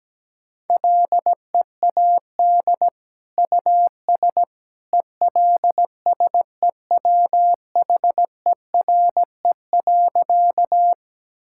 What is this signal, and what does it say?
Morse code, heard as a keyed tone, about 17 wpm, 700 Hz. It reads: LEAD US ELSEWHERE.
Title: Sentences from Top 1600 Words 17wpm